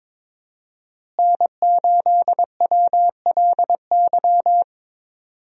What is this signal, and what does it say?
Morse code, heard as a keyed tone, about 22 wpm, 700 Hz. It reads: N8WLY
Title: USA Call Signs - 5 Characters - Repeated 3x - 22wpm